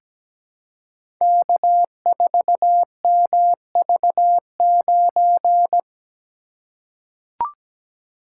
Morse code, heard as a keyed tone, about 17 wpm, 700 Hz. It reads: K4MV9
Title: Code Groups - 5 Letter:Numbers - 17wpm